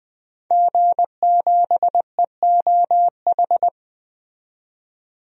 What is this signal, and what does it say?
Morse code, heard as a keyed tone, about 21 wpm, 700 Hz. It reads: G7EOH
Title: DXCC-100 Call Signs - 5 Characters - 20wpm